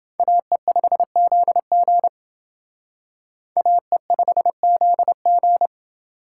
Morse code, heard as a keyed tone, about 28 wpm, 700 Hz. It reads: AE5ZG AE5ZG